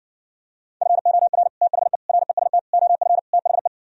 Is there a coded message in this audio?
Yes: Morse code from 0.8 to 3.7 s, heard as a keyed tone, about 68 wpm, 700 Hz. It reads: FOR THE REST OF THE